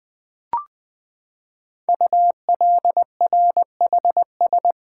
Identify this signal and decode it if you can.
Morse code, heard as a keyed tone, about 20 wpm, 700 Hz. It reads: ULRHS
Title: Code Groups - 5 Letters - 20wpm